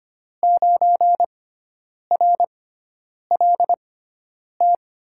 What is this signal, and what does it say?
Morse code, heard as a keyed tone, about 25 wpm, spaced as for 9 wpm, 700 Hz. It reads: 9RLT
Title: USA Call Signs - 6x Character Spacing - 25wpm